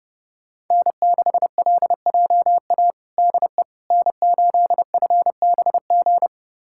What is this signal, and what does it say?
Morse code, heard as a keyed tone, about 30 wpm, 700 Hz. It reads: N6LJA DE N8FBG
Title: USA Callsign DE Callsign - 5 Characters - 30wpm